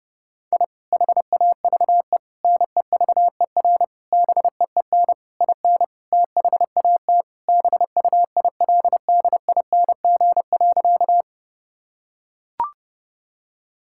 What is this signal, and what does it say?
Morse code, heard as a keyed tone, about 30 wpm, 700 Hz. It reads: I HAVE NEVER BEEN IN THAT BUILDING.
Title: Sentences from Top 1000 Words - Review Part 5 - 30wpm